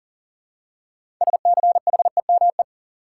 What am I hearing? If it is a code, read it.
Morse code, heard as a keyed tone, about 40 wpm, 700 Hz. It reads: SCHEME
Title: Six Letter Words - 40wpm